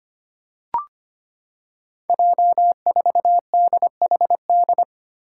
Morse code, heard as a keyed tone, about 25 wpm, 700 Hz. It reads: J4DHD
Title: Five Challenging Letter-Number Combinations 25wpm